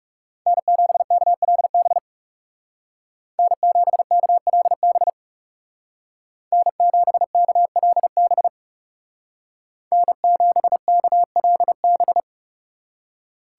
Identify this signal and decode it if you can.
Morse code, heard as a keyed tone, about 40 wpm, 700 Hz. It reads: N7KLB N7KLB N7KLB N7KLB